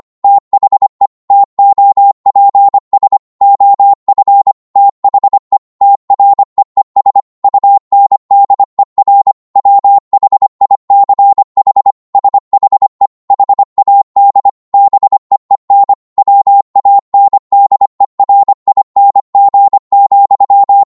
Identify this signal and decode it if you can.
Morse code, heard as a keyed tone, about 25 wpm, 800 Hz. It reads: THE TOPS OF THE TREES UNDER WHICH SHE HAD BEEN WANDERING,
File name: Morse / psg-25wpm